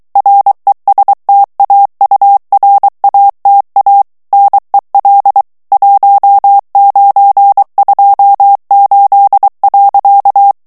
Morse code, heard as a keyed tone, about 23 wpm, 800 Hz. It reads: RESTAURATA NEL 1928.
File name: ITALIEN texte clair 1 vitesse 22 mots minute